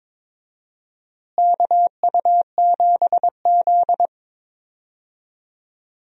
Morse code, heard as a keyed tone, about 21 wpm, 700 Hz. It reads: KU7Z
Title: USA Call Signs - 4 Characters - Repeated 3x - 22wpm